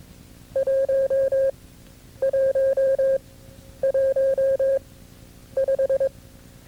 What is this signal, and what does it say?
Morse code, heard as a keyed tone, about 22 wpm, spaced as for 9 wpm, 540 Hz. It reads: 1115
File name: CW-practice/KB6MT-tape2-side1